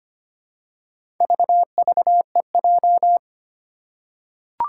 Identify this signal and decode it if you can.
Morse code, heard as a keyed tone, about 25 wpm, 700 Hz. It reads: VVEJ
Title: Four Challenging Letter-Number Combinations 25wpm